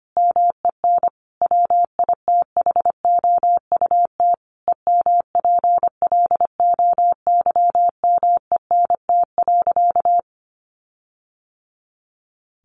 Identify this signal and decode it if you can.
Morse code, heard as a keyed tone, about 25 wpm, 700 Hz. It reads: MEN WITHOUT EMPLOYMENT.
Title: Sentences from Top 1500 Words 25wpm